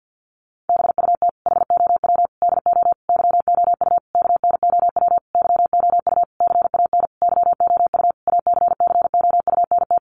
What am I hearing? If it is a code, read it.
Morse code, heard as a keyed tone, about 50 wpm, 700 Hz. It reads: BUT HOW DO YOU KNOW YOU CAN YOU ACCOUNT
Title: Sentences from Top 900 Words 50wpm